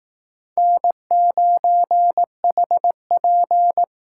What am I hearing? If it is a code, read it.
Morse code, heard as a keyed tone, about 19 wpm, 700 Hz. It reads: N9HP
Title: Speed Racing - USA Call Signs 15wpm